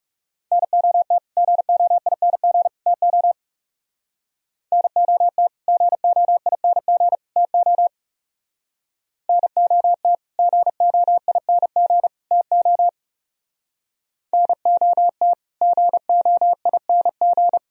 Morse code, heard as a keyed tone, about 37 wpm, 700 Hz. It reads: NOT GOING TO NOT GOING TO NOT GOING TO NOT GOING
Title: Speed-Racing - Sets of 3 Words 30wpm